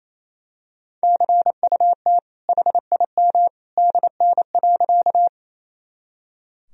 Morse code, heard as a keyed tone, about 28 wpm, 700 Hz. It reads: CUT HIM DN.
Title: Sentences from Top 300 Words and Abbreviations 28wpm